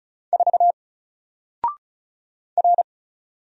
Morse code, heard as a keyed tone, about 36 wpm, 700 Hz. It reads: 4 R
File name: Single-Letter-Number-35wpm